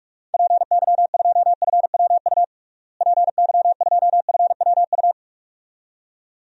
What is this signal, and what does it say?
Morse code, heard as a keyed tone, about 46 wpm, 700 Hz. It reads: PY2FWU PY2FWU
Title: DXCC-100 Call Signs - 6 Characters - Repeated 2x - 45wpm